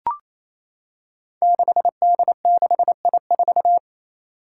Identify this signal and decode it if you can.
Morse code, heard as a keyed tone, about 28 wpm, 700 Hz. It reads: 6D6I4
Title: Five Challenging Letter-Number Combinations 28wpm